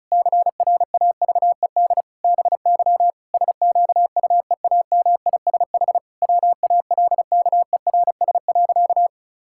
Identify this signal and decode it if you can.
Morse code, heard as a keyed tone, about 36 wpm, 700 Hz. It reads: CRAVED BY SQUEAMISH WALKERS.
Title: Pangram Challenge v2 - 35wpm